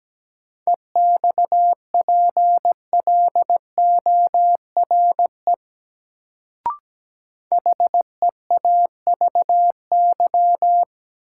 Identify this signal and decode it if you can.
Morse code, heard as a keyed tone, about 17 wpm, 700 Hz. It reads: EXPLORE HEAVY
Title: Sentences from Top 1100 Words - New Word Drill Alt - 17wpm